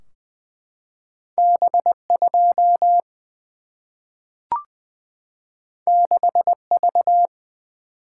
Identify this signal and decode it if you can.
Morse code, heard as a keyed tone, about 20 wpm, 700 Hz. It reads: B 2 6 V
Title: Single-Letter Single-Number 20wpm